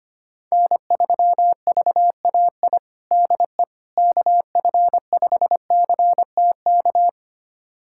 Morse code, heard as a keyed tone, about 25 wpm, 700 Hz. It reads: N3VAI DE KF5CTK
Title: USA Callsign DE Callsign 25wpm